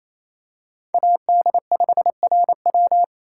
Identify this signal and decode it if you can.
Morse code, heard as a keyed tone, about 28 wpm, 700 Hz. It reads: AD5RW
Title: USA Call Signs - 5 Characters - 4x Character Spacing - 28wpm